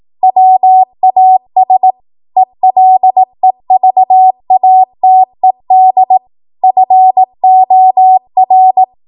Morse code, heard as a keyed tone, about 18 wpm, 750 Hz. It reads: WAS ELEVATED FOR